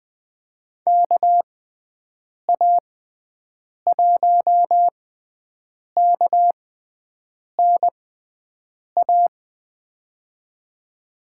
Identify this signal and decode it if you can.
Morse code, heard as a keyed tone, about 20 wpm, spaced as for 7 wpm, 700 Hz. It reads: KA1KNA
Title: USA Call Signs - 6 Characters - 6x Character Spacing - 20wpm